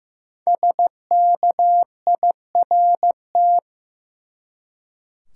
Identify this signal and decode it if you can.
Morse code, heard as a keyed tone, about 15 wpm, 700 Hz. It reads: SKIRT
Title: Five Letter Words - 15wpm